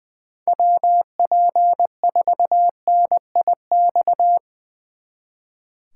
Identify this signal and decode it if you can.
Morse code, heard as a keyed tone, about 22 wpm, 700 Hz. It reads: WP4NIX
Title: Speed Racing - USA 2x3 Call Signs 20wpm